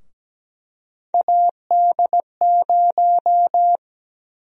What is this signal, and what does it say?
Morse code, heard as a keyed tone, about 17 wpm, 700 Hz. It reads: AD0
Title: USA Call Sign Prefixes - 17wpm